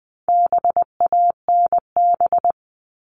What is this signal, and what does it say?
Morse code, heard as a keyed tone, about 20 wpm, 700 Hz. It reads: BANB
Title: All Cut Numbers - 4-Digit 20wpm